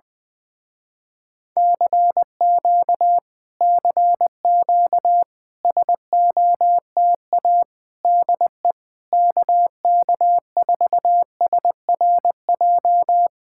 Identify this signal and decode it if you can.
Morse code, heard as a keyed tone, about 20 wpm, 700 Hz. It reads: CQ CQ SOTA DE KK4SRJ
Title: Speed-Racing Common QSO Phrases Encore - 17wpm